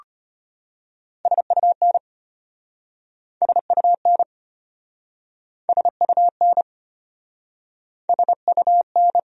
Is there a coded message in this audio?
Yes: Morse code from 1.3 to 9.2 s, heard as a keyed tone, about 33 wpm, 700 Hz. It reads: SUN SUN SUN SUN